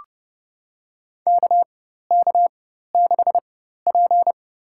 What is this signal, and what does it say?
Morse code, heard as a keyed tone, about 29 wpm, spaced as for 14 wpm, 700 Hz. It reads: KK6P
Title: USA Call Signs - 4 Characters - 4x Character Spacing - 30wpm